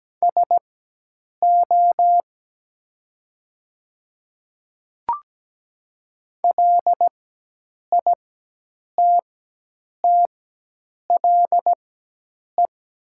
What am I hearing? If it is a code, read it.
Morse code, heard as a keyed tone, about 17 wpm, spaced as for 8 wpm, 700 Hz. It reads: SO LITTLE
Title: 4x Character Spacing - Top 200 Words - 17wpm